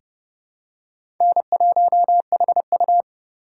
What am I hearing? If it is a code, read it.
Morse code, heard as a keyed tone, about 30 wpm, 700 Hz. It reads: N1HU
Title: USA Call Signs - 4 Characters - 4x Character Spacing - 30wpm